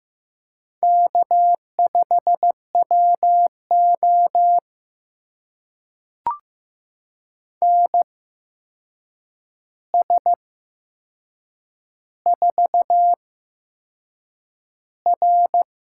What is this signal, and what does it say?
Morse code, heard as a keyed tone, about 15 wpm, spaced as for 6 wpm, 700 Hz. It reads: K5WO NS4R
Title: USA Call Signs - 4 Characters - 8x Character Spacing - 15wpm